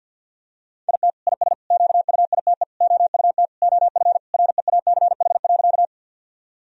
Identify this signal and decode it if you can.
Morse code, heard as a keyed tone, about 48 wpm, 700 Hz. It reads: IT IS QUITE OUT OF REACH.
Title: Sentences from Top 1000 Words - Review Part 2 - 50wpm